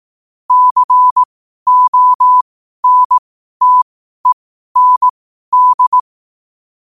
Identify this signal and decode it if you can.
Morse code, heard as a keyed tone, about 18 wpm, spaced as for 12 wpm, 1000 Hz. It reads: CONTEND
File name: qotdfiles / 12wpm